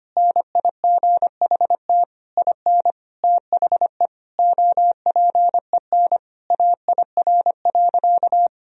Morse code, heard as a keyed tone, about 25 wpm, 700 Hz. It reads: NIGHT IN THE OPEN AIR.
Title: Sentences from Top 1000 Words - Review Part 1 - 25wpm